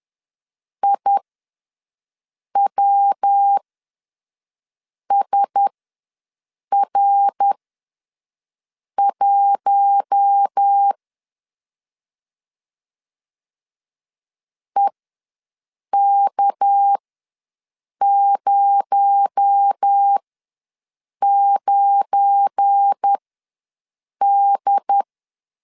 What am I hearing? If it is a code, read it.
Morse code, heard as a keyed tone, about 10 wpm, spaced as for 5 wpm, 800 Hz. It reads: IWSR1 EK09D
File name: LEZIONI / 7 GF9Z / 7 GF9 ws 10wpm 4034